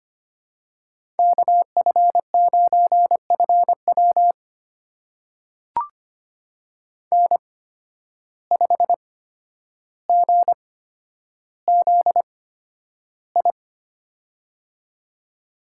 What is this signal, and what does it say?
Morse code, heard as a keyed tone, about 25 wpm, spaced as for 11 wpm, 700 Hz. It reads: KF9FW N5GZI
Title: USA Call Signs - 5 Characters - 8x Character Spacing - 25wpm